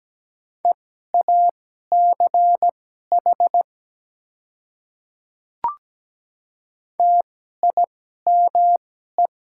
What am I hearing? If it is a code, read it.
Morse code, heard as a keyed tone, about 17 wpm, spaced as for 12 wpm, 700 Hz. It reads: EACH TIME
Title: 2x Character Spacing - Top 100 Words - 17wpm